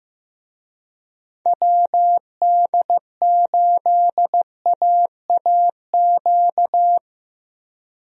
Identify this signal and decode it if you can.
Morse code, heard as a keyed tone, about 15 wpm, 700 Hz. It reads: WD8AAQ
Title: USA Call Signs - 6 Characters - 15wpm